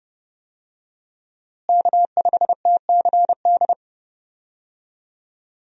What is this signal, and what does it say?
Morse code, heard as a keyed tone, about 30 wpm, 700 Hz. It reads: K5TCD
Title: USA Call Signs - 5 Characters - Repeated 2x - 30wpm